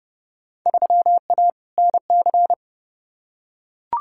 Morse code, heard as a keyed tone, about 30 wpm, 700 Hz. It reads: 3A NC
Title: Field Day Exchanges - 30wpm